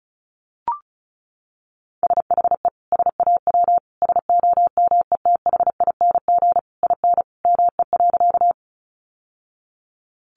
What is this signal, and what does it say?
Morse code, heard as a keyed tone, about 34 wpm, 700 Hz. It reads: SHE SAW SOMETHING IN ME.
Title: Sentences from Top 600 Words 35wpm